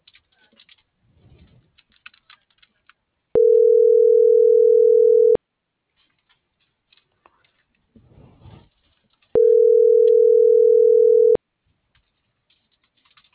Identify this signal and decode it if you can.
Ringback tone (North America), 440 + 480 Hz.